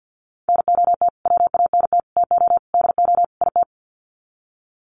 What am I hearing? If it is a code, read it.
Morse code, heard as a keyed tone, about 50 wpm, 700 Hz. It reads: NOT WANT TO DO IT
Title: Sets of 5 Words Encore 50wpm